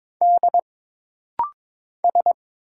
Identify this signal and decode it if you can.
Morse code, heard as a keyed tone, about 22 wpm, 700 Hz. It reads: D S
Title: Single Letter-Number - Rapid-Fire 22wpm